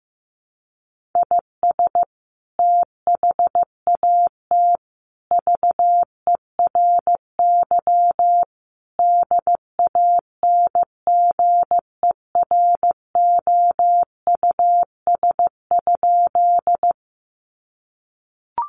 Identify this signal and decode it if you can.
Morse code, heard as a keyed tone, about 15 wpm, 700 Hz. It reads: IS THAT VERY DANGEROUS?
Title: Sentences from Top 1200 Words 15wpm